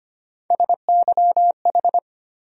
Morse code, heard as a keyed tone, about 25 wpm, 700 Hz. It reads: SYH
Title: USA Call Sign Postfixes - 3 Letters 25wpm